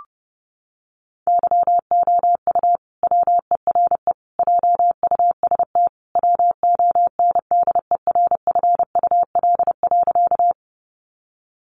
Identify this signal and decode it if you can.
Morse code, heard as a keyed tone, about 30 wpm, 700 Hz. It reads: YOU WERE JUST WONDERFUL.
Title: Sentences from Top 1100 Words 30wpm